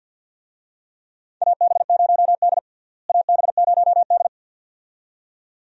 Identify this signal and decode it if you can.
Morse code, heard as a keyed tone, about 53 wpm, 700 Hz. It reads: AB0D AB0D
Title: USA Call Signs - 4 Characters - Repeated 2x - 50wpm